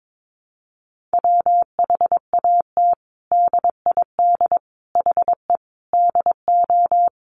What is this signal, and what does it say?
Morse code, heard as a keyed tone, about 22 wpm, 700 Hz. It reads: WHAT DID HE DO
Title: Sentences from Top 100 Words 22wpm